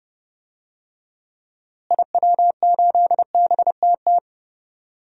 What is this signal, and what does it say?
Morse code, heard as a keyed tone, about 30 wpm, 700 Hz. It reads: IW8BTT
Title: DXCC-100 Call Signs - 6 Characters - Repeated 3x - 30wpm